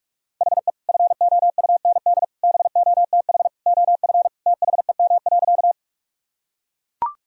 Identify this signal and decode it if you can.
Morse code, heard as a keyed tone, about 41 wpm, 700 Hz. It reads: HE FOUND BOTH OF THEM.
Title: Sentences from Top 1000 Words - Review Part 1 - 45wpm